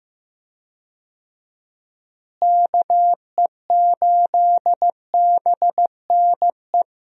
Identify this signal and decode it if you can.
Morse code, heard as a keyed tone, about 16 wpm, 700 Hz. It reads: KE8BNE
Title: USA Call Signs - 6 Characters - Repeated 2x - 15wpm